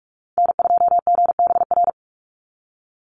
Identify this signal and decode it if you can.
Morse code, heard as a keyed tone, about 44 wpm, 700 Hz. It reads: N2GBR
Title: NA SOTA Activator Callsigns 45wpm